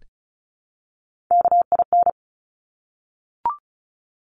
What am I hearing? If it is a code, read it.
Morse code, heard as a keyed tone, about 36 wpm, 700 Hz. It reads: KIN
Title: Top Three Letter Combinations in Words 35wpm